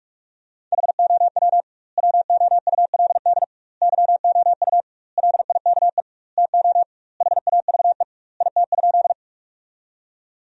Morse code, heard as a keyed tone, about 46 wpm, 700 Hz. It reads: HOW WOULD YOU LIKE TO HAVE IT?
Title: 2x Word-Spacing - Sentences from Top 100 Words - 45wpm